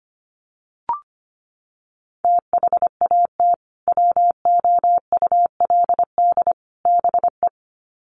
Morse code, heard as a keyed tone, about 25 wpm, 700 Hz. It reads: THAT WOULD BE
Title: Sets of 3 Words 25wpm